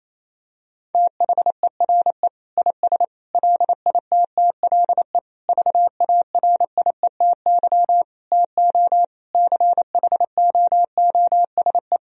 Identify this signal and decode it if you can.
Morse code, heard as a keyed tone, about 28 wpm, 700 Hz. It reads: THERE IS LITTLE VARIETY TO CHOOSE
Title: Sentences from Top 1000 Words - Review Part 4 - 28wpm